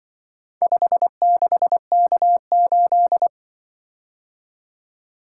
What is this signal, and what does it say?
Morse code, heard as a keyed tone, about 25 wpm, 700 Hz. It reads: 56K8
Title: Speed-Racing - Four Letter-Number Combinations - 20wpm